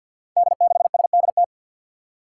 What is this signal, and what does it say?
Morse code, heard as a keyed tone, about 52 wpm, 700 Hz. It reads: D6SDT
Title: Five Challenging Letter-Number Combinations 50wpm